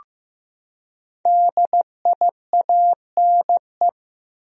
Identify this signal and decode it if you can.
Morse code, heard as a keyed tone, about 15 wpm, 700 Hz. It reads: DIANE
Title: USA Common Names - Max 5 Letters - 15wpm